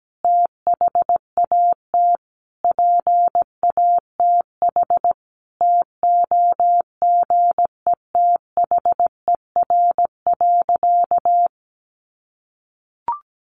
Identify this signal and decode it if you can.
Morse code, heard as a keyed tone, about 17 wpm, 700 Hz. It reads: THAT PATH TOGETHER.
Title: Sentences from Top 1400 Words 17wpm